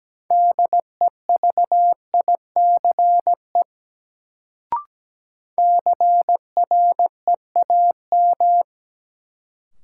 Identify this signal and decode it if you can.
Morse code, heard as a keyed tone, about 17 wpm, 700 Hz. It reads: DEVICE CREAM
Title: Sentences from Top 1500 Words - New Word Drill - 17wpm